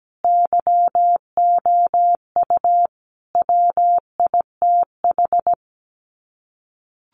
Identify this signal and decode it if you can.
Morse code, heard as a keyed tone, about 17 wpm, 700 Hz. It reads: YOU WITH
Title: Sets of 5 Words Encore2 17wpm